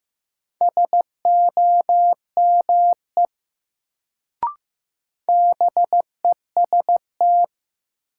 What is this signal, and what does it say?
Morse code, heard as a keyed tone, about 15 wpm, 700 Hz. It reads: SOME BEST
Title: Sets of 2 Words - New Word Drill - 15wpm